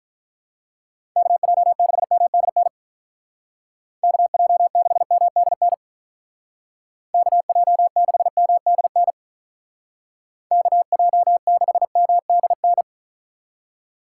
Morse code, heard as a keyed tone, about 43 wpm, 700 Hz. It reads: KJ6MDN KJ6MDN KJ6MDN KJ6MDN
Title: Speed Racing - USA Call Signs 35wpm